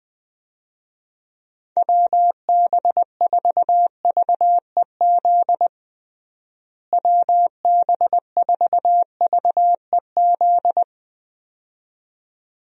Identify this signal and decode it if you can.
Morse code, heard as a keyed tone, about 20 wpm, 700 Hz. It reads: WB4VEZ WB4VEZ
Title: USA Call Signs - 6 Characters - Repeated 2x - 20wpm